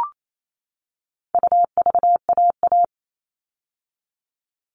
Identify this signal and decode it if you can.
Morse code, heard as a keyed tone, about 28 wpm, 700 Hz. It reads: UVAA
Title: All Cut Numbers - 4-Digit 28wpm